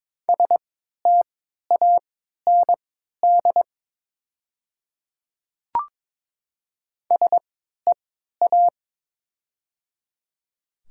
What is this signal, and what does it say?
Morse code, heard as a keyed tone, about 23 wpm, spaced as for 13 wpm, 700 Hz. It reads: STAND SEA
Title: 3x Character Spacing - Top 300 Words - 22wpm